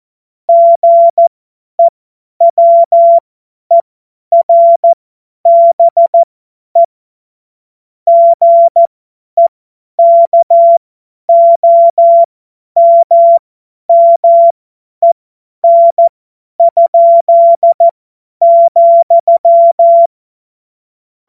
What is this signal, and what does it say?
Morse code, heard as a keyed tone, about 14 wpm, spaced as for 10 wpm, 680 Hz. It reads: GEWERBE GEKOMMEN?,
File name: Morsekurs / Witze / Witz 14.November 2023 50 F70